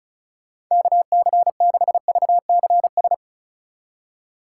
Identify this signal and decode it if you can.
Morse code, heard as a keyed tone, about 45 wpm, 700 Hz. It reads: KC6VCS